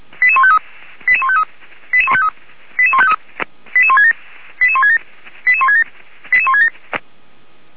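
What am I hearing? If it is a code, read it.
Five-tone selective call (ZVEI), heard as quick runs of five tones, tones of about 70 ms. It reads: 88153 88153 88153 88167 88167 88167 88167
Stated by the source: ZVEI five-tone selective call 88133 received by SDR.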